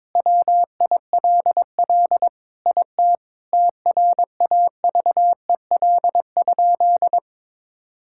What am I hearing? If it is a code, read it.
Morse code, heard as a keyed tone, about 22 wpm, 700 Hz. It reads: WILL IT TRAVEL?
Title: Sentences from Top 1000 Words - Review Part 6 - 22wpm